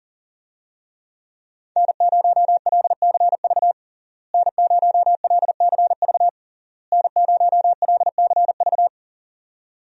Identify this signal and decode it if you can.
Morse code, heard as a keyed tone, about 40 wpm, 700 Hz. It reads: N0LCV N0LCV N0LCV
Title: USA Call Signs - 5 Characters - Repeated 3x - 40wpm